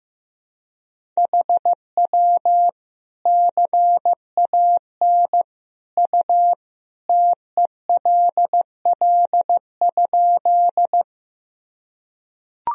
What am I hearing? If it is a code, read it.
Morse code, heard as a keyed tone, about 15 wpm, 700 Hz. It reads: HW CAN U TELL?
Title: Sentences from Top 200 Words and Abbreviations 15wpm